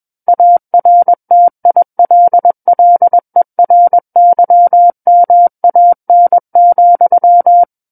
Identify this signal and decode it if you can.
Morse code, heard as a keyed tone, about 21 wpm, 700 Hz. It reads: ARTILLERYMAN,
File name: Book One / 12-Chapter 12 - 21 WPM